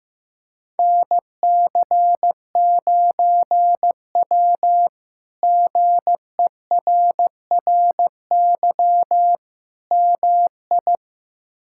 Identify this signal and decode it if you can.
Morse code, heard as a keyed tone, about 16 wpm, 700 Hz. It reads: NC9W GERRY MI
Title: K1USN SST QSO Practice - 15wpm